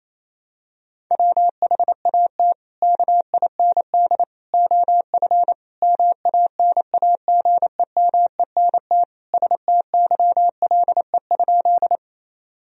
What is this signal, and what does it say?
Morse code, heard as a keyed tone, about 28 wpm, 700 Hz. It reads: WHAT KIND OF MANAGEMENT STYLE?
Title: Sentences from Top 1000 Words - Review Part 4 - 28wpm